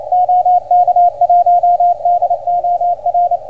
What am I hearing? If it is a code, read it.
Morse code, heard as a keyed tone, about 27 wpm, 690 Hz, with radio noise. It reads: OK1DOR